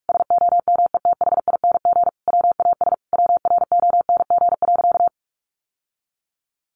Morse code, heard as a keyed tone, about 44 wpm, 700 Hz. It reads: SOMETHING WAS WRONG.
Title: Sentences from Top 1200 Words 45wpm